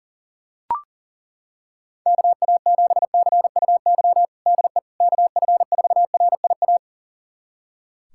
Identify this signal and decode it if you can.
Morse code, heard as a keyed tone, about 37 wpm, 700 Hz. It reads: KA7CUY DE KF4RIA
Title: USA Callsign DE Callsign 40wpm